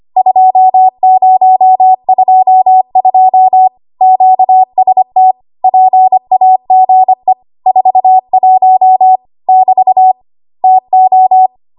Morse code, heard as a keyed tone, about 25 wpm, 750 Hz. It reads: 2022 QST PAGE 41 <BT> TO
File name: Archive / 25 WPM / 25WPM